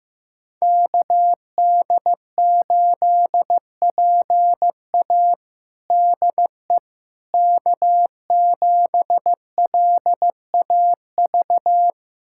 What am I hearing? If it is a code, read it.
Morse code, heard as a keyed tone, about 15 wpm, 700 Hz. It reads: KD8PA DE K7LAV
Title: USA Callsign DE Callsign - 5 Characters - 15wpm